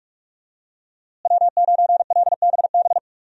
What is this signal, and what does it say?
Morse code, heard as a keyed tone, about 46 wpm, 700 Hz. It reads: W9LBB
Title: USA Call Signs - 5 Characters - Repeated 3x - 45wpm